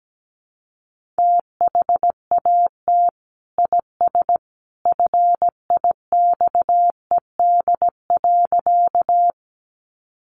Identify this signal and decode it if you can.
Morse code, heard as a keyed tone, about 17 wpm, 700 Hz. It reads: THAT IS FIXED.
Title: Sentences from Top 1100 Words 17wpm